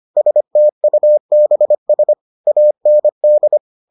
Morse code, heard as a keyed tone, about 25 wpm, 600 Hz. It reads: STUBS AND